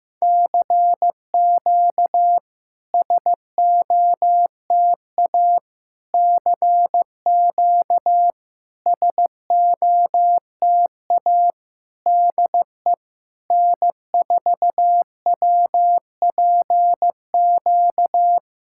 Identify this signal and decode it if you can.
Morse code, heard as a keyed tone, about 15 wpm, 700 Hz. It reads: CQ SOTA CQ SOTA DE N4WPQ
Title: SOTA QSO Practice 15wpm